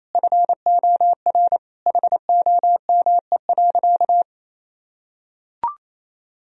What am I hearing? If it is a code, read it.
Morse code, heard as a keyed tone, about 28 wpm, 700 Hz. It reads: FOR HOME.
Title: Sentences from Top 1000 Words - Review Part 1 - 28wpm